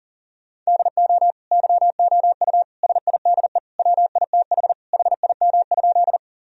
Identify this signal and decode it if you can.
Morse code, heard as a keyed tone, about 40 wpm, 700 Hz. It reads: DO YOU SIDE WITH HIM?
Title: Sentences from Top 1000 Words - Review Part 1 - 40wpm